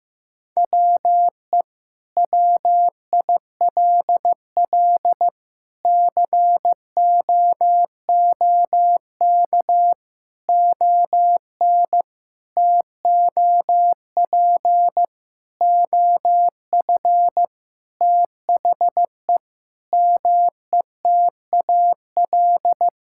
Morse code, heard as a keyed tone, about 15 wpm, 700 Hz. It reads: WE WILL COOK ON TOP OF THE METAL
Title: Sentences from Top 1000 Words - Review Part 6 - 15wpm